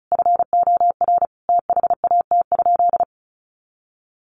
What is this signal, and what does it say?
Morse code, heard as a keyed tone, about 34 wpm, 700 Hz. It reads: FOR THAT?
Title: Sentences from Top 300 Words 35wpm